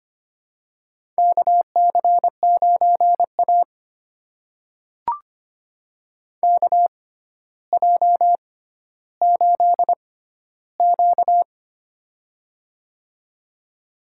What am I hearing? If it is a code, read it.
Morse code, heard as a keyed tone, about 25 wpm, spaced as for 13 wpm, 700 Hz. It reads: KC9A KJ8Q
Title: USA Call Signs - 4 Characters - 6x Character Spacing - 25wpm